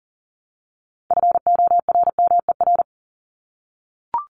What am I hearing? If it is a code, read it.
Morse code, heard as a keyed tone, about 40 wpm, 700 Hz. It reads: FORMER
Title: LY Suffix - Top 100 Words with Base Words 40wpm